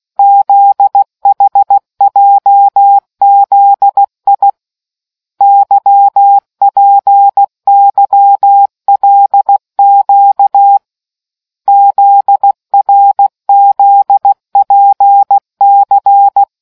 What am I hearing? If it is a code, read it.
Morse code, heard as a keyed tone, about 16 wpm, 800 Hz. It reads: ZHJZI YPYLQ ZRZPC